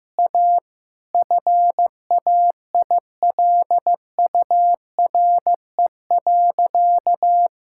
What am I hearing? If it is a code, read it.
Morse code, heard as a keyed tone, about 15 wpm, 700 Hz. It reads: A FAILURE.
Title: Sentences from Top 1000 Words - Review Part 6 - 15wpm